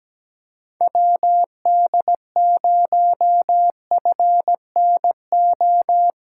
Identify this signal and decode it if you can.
Morse code, heard as a keyed tone, about 17 wpm, 700 Hz. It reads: WD0FNO
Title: USA Callsign DE Callsign - 6 Characters - 17wpm